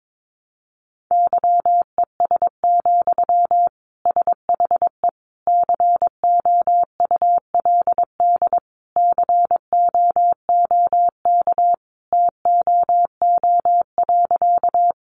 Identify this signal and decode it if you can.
Morse code, heard as a keyed tone, about 22 wpm, 700 Hz. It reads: YES, SHE COULD COOK TOO.
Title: Sentences from Top 700 Words 22wpm